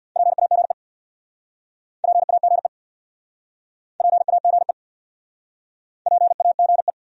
Morse code, heard as a keyed tone, about 67 wpm, 700 Hz. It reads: PAGE PAGE PAGE PAGE